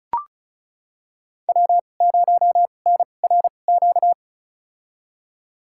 Morse code, heard as a keyed tone, about 36 wpm, spaced as for 25 wpm, 700 Hz. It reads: W0NRQ
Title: USA Call Signs - 5 Characters - 2x Character Spacing - 35wpm